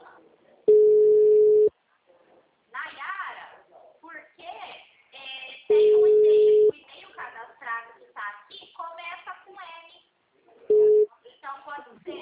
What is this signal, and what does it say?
Ringback tone (Europe, 425 Hz), with speech.
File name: out- -2003- - - .126